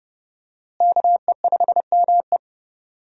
Morse code, heard as a keyed tone, about 30 wpm, 700 Hz. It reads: KE5ME
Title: USA Call Signs - 5 Characters - Repeated 3x - 30wpm